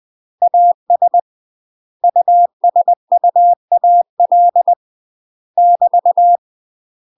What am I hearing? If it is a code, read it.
Morse code, heard as a keyed tone, about 20 wpm, 700 Hz. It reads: AS USUAL <BT>
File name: UK MASTER QSOs TRAINING / 20WPM / UK MASTER QSOs 0072